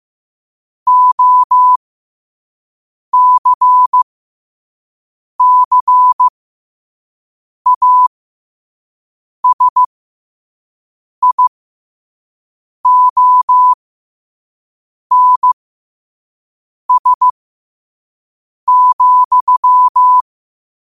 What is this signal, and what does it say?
Morse code, heard as a keyed tone, about 15 wpm, spaced as for 5 wpm, 1000 Hz. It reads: OCCASIONS,
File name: qotdfiles / 5wpm